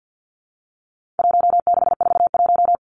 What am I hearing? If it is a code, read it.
Morse code, heard as a keyed tone, about 50 wpm, 700 Hz. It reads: 1641
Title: Numbers - Contest Style - 4-Digit 50wpm